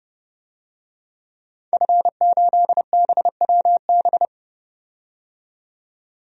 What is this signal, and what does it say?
Morse code, heard as a keyed tone, about 30 wpm, 700 Hz. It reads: F8BWB
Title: DXCC-100 Call Signs - 5 Characters - 30wpm